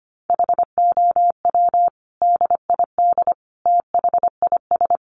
Morse code, heard as a keyed tone, about 25 wpm, 700 Hz. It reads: HOW DID THIS
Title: Sentences from Top 1400 Words 25wpm